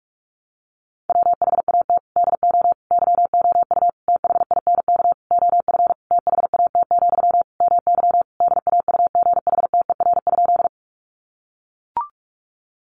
Morse code, heard as a keyed tone, about 44 wpm, 700 Hz. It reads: WHAT DO YOU THINK OF THAT, MY DAUGHTER?
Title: Sentences from Top 1500 Words 45wpm